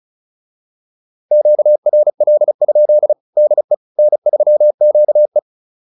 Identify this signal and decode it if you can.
Morse code, heard as a keyed tone, about 35 wpm, 600 Hz. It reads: QRL? DE N3QE
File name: QSO 101 35